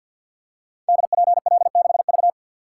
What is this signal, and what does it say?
Morse code, heard as a keyed tone, about 47 wpm, 700 Hz. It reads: DPL6V
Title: Code Groups - 5 Letter:Numbers - 50wpm